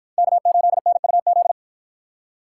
Morse code, heard as a keyed tone, about 47 wpm, 700 Hz. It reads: K8NUZ